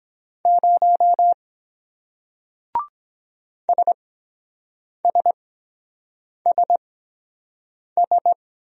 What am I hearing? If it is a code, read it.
Morse code, heard as a keyed tone, about 23 wpm, spaced as for 7 wpm, 700 Hz. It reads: 0 SSSS